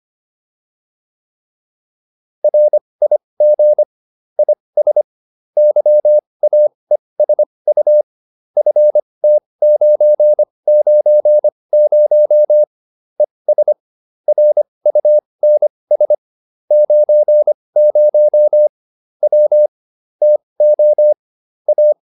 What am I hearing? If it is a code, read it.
Morse code, heard as a keyed tone, about 25 wpm, spaced as for 20 wpm, 600 Hz. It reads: RIG IS YAESU FT990 ES RUNS 90 W TO A